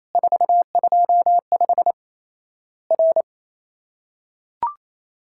Morse code, heard as a keyed tone, about 28 wpm, 700 Hz. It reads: 425 R
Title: Worked All Europe DX Contest - QTC Exchange - 28wpm